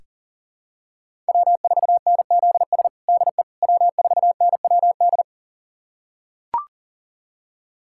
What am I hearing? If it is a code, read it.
Morse code, heard as a keyed tone, about 40 wpm, 700 Hz. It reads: W4NZS DE W4NWD
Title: USA Callsign DE Callsign - 5 Characters - 40wpm